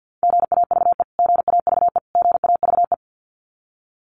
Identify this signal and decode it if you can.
Morse code, heard as a keyed tone, about 50 wpm, 700 Hz. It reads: GAVE GAVE GAVE
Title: Top 500 Words Repeated 50wpm